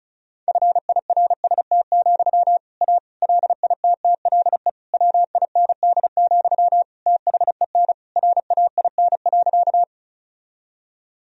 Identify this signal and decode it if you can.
Morse code, heard as a keyed tone, about 36 wpm, 700 Hz. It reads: FIRST, A LITTLE WIND, THEN RAIN.
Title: Sentences from Top 1000 Words - Review Part 2 - 35wpm